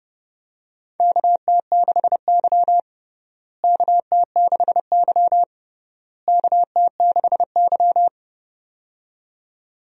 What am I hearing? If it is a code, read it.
Morse code, heard as a keyed tone, about 32 wpm, 700 Hz. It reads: KT6Y KT6Y KT6Y
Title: USA Call Signs - 4 Characters - Repeated 3x - 30wpm